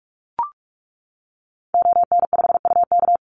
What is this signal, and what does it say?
Morse code, heard as a keyed tone, about 44 wpm, 700 Hz. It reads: ON5UK
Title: Call Signs - 45wpm